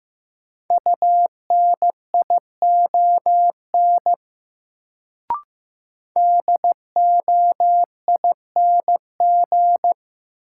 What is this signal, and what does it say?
Morse code, heard as a keyed tone, about 15 wpm, 700 Hz. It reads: UNION DOING
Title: Sentences from Top 1400 Words - New Word Drill - 15wpm